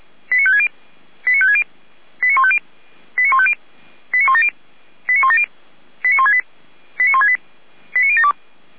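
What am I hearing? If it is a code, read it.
Five-tone selective call (ZVEI), heard as quick runs of five tones, tones of about 70 ms. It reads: 89560 89560 89150 89150 89170 89170 89168 89168 89072
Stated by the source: ZVEI five-tone call received by SDR, recorded by the Orlamünde fire brigade (Feuerwehr Orlamünde).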